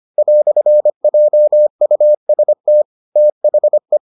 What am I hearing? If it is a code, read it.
Morse code, heard as a keyed tone, about 25 wpm, 600 Hz. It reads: "JUST THE